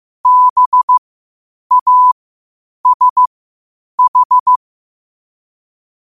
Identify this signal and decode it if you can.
Morse code, heard as a keyed tone, about 15 wpm, spaced as for 8 wpm, 1000 Hz. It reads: BASH
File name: qotdfiles / 8wpm